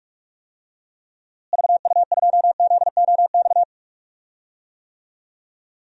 Morse code, heard as a keyed tone, about 42 wpm, 700 Hz. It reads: VU2ZOX
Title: DXCC-100 Call Signs - 6 Characters - Repeated 2x - 45wpm